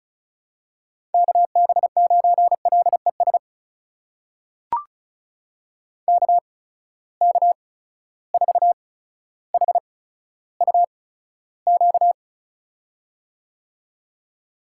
Morse code, heard as a keyed tone, about 35 wpm, spaced as for 15 wpm, 700 Hz. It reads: KB9LES KK4HUQ
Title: USA Call Signs - 6 Characters - 8x Character Spacing - 35wpm